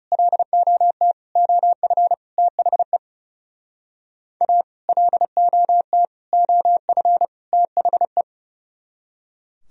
Morse code, heard as a keyed tone, about 40 wpm, 700 Hz. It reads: LOT OF THE A LOT OF THE